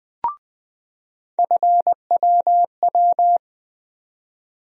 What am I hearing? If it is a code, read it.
Morse code, heard as a keyed tone, about 20 wpm, 700 Hz. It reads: FWW
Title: USA Call Sign Postfixes - 3 Letters 20wpm